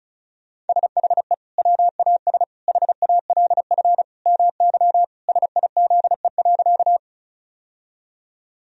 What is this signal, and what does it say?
Morse code, heard as a keyed tone, about 34 wpm, 700 Hz. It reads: SHE WAS HALF MY SIZE.
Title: Sentences from Top 1000 Words - Review Part 2 - 35wpm